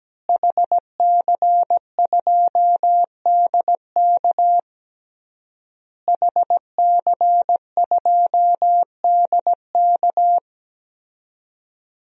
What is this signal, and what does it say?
Morse code, heard as a keyed tone, about 17 wpm, 700 Hz. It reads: HC2DK HC2DK
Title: DXCC-100 Call Signs - 5 Characters - Repeated 3x - 17wpm